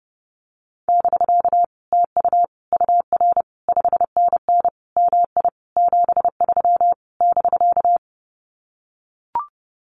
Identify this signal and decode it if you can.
Morse code, heard as a keyed tone, about 30 wpm, 700 Hz. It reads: <BK> TU UR 5NN MI 73 <BK>
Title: Common QSO Phrases Encore 30wpm